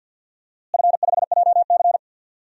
Morse code, heard as a keyed tone, about 50 wpm, 700 Hz. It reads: F5JC
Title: DXCC-100 Call Signs - 4 Characters - Repeated 3x - 50wpm